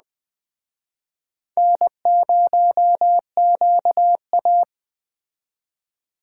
Morse code, heard as a keyed tone, about 21 wpm, 700 Hz. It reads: N0QA